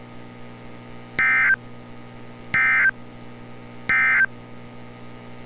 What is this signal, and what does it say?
Emergency Alert System (SAME) tones, heard as harsh buzzing bursts. It is heard from 1.1 to 4.3 s.
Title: NOAA Weather Radio - Altoona, PA